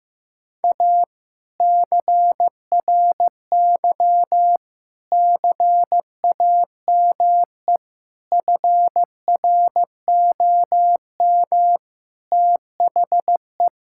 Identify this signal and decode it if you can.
Morse code, heard as a keyed tone, about 15 wpm, 700 Hz. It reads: A CRY CAME FROM THE
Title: Sentences from Top 1000 Words - Review Part 5 - 15wpm